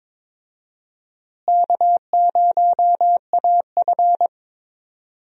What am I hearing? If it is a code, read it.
Morse code, heard as a keyed tone, about 22 wpm, 700 Hz. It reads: K0AF
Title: USA Call Signs - 4 Characters - Repeated 2x - 22wpm